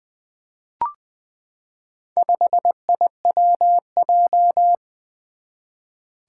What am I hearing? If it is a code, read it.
Morse code, heard as a keyed tone, about 20 wpm, 700 Hz. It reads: 5IWJ
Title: Four Challenging Letter-Number Combinations 20wpm